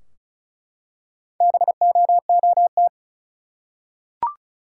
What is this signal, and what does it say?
Morse code, heard as a keyed tone, about 36 wpm, 700 Hz. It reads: BOOT
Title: Four Letter Words - 35wpm